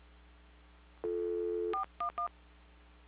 DTMF, heard as telephone touch tones, tones of about 110 ms. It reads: 411
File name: recordings / PortlandUnion Millennium